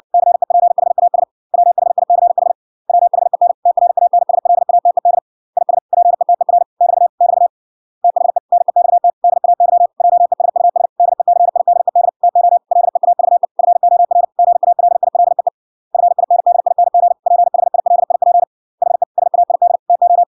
Morse code, heard as a keyed tone, about 60 wpm, 700 Hz. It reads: CEPHAS WHICH WHEN TRANSLATED IS PETER <BT> <BT> THE NEXT DAY JESUS DECIDED TO LEAVE FOR GALILEE FINDING PHILIP HE SAID TO